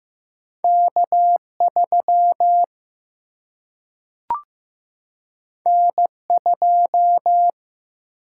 Morse code, heard as a keyed tone, about 15 wpm, 700 Hz. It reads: K3 N2
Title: Single-Letter Single-Number 15wpm